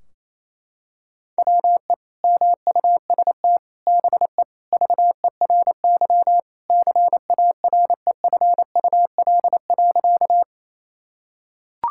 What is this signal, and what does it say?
Morse code, heard as a keyed tone, about 28 wpm, 700 Hz. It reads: WE MUST BE VERY CAREFUL.
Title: Sentences from Top 1600 Words 28wpm